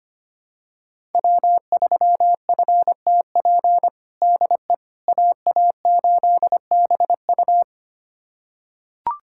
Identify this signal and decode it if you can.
Morse code, heard as a keyed tone, about 25 wpm, 700 Hz. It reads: W3FTP DE AA8BU
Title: USA Callsign DE Callsign - 5 Characters - 25wpm